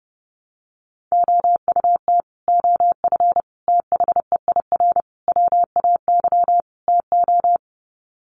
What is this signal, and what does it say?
Morse code, heard as a keyed tone, about 30 wpm, 700 Hz. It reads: OUT OF THEIR WAY TO
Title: Sets of 5 Words Encore2 30wpm